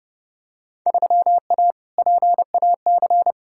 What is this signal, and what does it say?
Morse code, heard as a keyed tone, about 30 wpm, 700 Hz. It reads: 3A PAC
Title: Field Day Exchanges - 30wpm